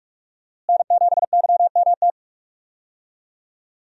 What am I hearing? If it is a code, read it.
Morse code, heard as a keyed tone, about 41 wpm, 700 Hz. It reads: N7YMT
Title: USA Call Signs - 5 Characters - Repeated 2x - 45wpm